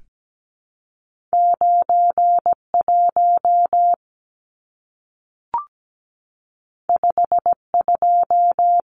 Morse code, heard as a keyed tone, about 17 wpm, 700 Hz. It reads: 91 52
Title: Numbers - 2-Digit 17wpm